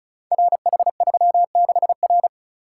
Morse code, heard as a keyed tone, about 36 wpm, 700 Hz. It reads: RH36R
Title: Code Groups - 5 Letter:Numbers - 35wpm